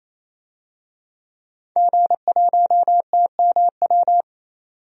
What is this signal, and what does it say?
Morse code, heard as a keyed tone, about 28 wpm, 700 Hz. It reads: G1TMW
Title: DXCC-100 Call Signs - 5 Characters - 28wpm